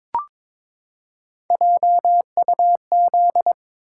Morse code, heard as a keyed tone, about 22 wpm, 700 Hz. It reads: JUZ
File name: practice / 22wpm / Three Letter-Number Combinations - 22wpm